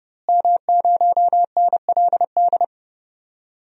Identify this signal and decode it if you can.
Morse code, heard as a keyed tone, about 30 wpm, 700 Hz. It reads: M0NLD
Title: DXCC-100 Call Signs - 5 Characters - Repeated 2x - 30wpm